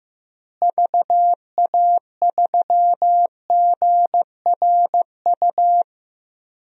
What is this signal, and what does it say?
Morse code, heard as a keyed tone, about 15 wpm, 700 Hz. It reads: VA3GRU
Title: DXCC-100 Call Signs - 6 Characters - Repeated 2x - 15wpm